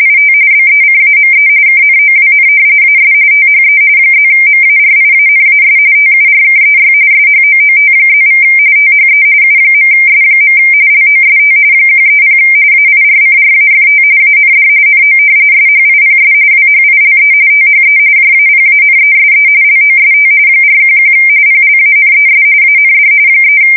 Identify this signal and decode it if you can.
RTTY, about 45.45 baud. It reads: IT BECAME KNOWN AS THE +MARIANAS TURKEY SHOOT+= THE U.S. LOST 130 AIRCRAFT AND NO SHIPS WHILE THE JAPANESE LOST 411 PLANES AND 3
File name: fsk170 / usnavy4